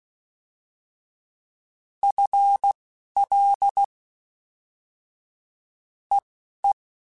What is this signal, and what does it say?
Morse code, heard as a keyed tone, about 16 wpm, spaced as for 12 wpm, 800 Hz. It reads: FL EE
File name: FL / FL mp3s / Fast FL @10